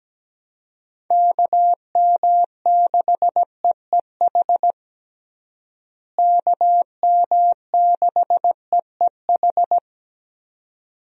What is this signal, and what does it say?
Morse code, heard as a keyed tone, about 17 wpm, 700 Hz. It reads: KM6EEH KM6EEH
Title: USA Call Signs - 6 Characters - Repeated 2x - 17wpm